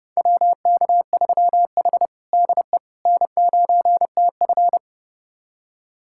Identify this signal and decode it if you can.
Morse code, heard as a keyed tone, about 32 wpm, 700 Hz. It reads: WK3H DE N9TF
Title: USA Callsign DE Callsign - 4 Characters - 30wpm